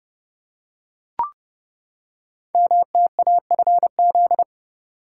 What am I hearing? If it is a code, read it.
Morse code, heard as a keyed tone, about 30 wpm, 700 Hz. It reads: MTAFZ
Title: Code Groups - 5 Letters - 30wpm